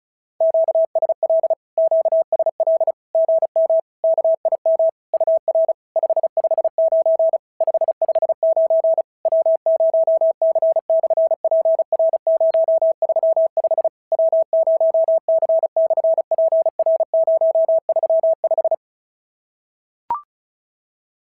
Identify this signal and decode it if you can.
Morse code, heard as a keyed tone, about 36 wpm, 690 Hz. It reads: QSL QSL GM KIM UR 559 559 W0C/PR035 W0C/PR035
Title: SOTA QSO Practice 35wpm